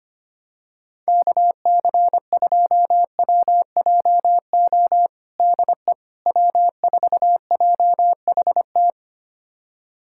Morse code, heard as a keyed tone, about 25 wpm, 700 Hz. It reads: KC2WJO DE W4JHT
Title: USA Callsign DE Callsign 25wpm